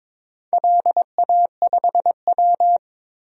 Morse code, heard as a keyed tone, about 22 wpm, 700 Hz. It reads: LA5W
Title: DXCC-100 Call Signs - 4 Characters - 22wpm